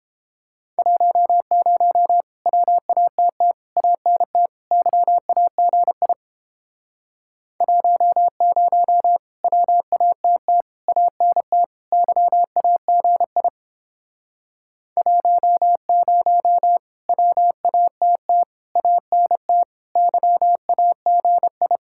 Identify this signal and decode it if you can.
Morse code, heard as a keyed tone, about 31 wpm, 700 Hz. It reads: 10 WATT ANT YAGI 10 WATT ANT YAGI 10 WATT ANT YAGI